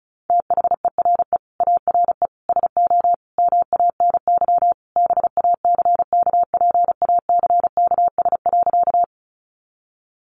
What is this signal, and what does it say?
Morse code, heard as a keyed tone, about 34 wpm, 700 Hz. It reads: THERE ARE SO MANY BACKPACKS.
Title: Additional Easy Phrases 35wpm